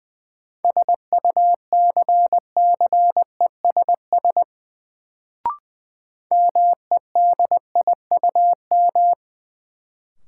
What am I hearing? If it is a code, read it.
Morse code, heard as a keyed tone, about 20 wpm, 700 Hz. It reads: SUCCESS MEDIUM
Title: Sentences from Top 900 Words - New Word Drill - 20wpm